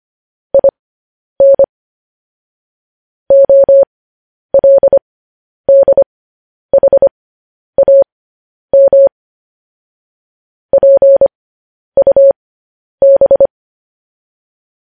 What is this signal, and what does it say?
Morse code, heard as a keyed tone, about 25 wpm, spaced as for 10 wpm, 550 Hz. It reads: IN OLDHAM PUB
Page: news-headlines-in-morse-code-at-10-wpm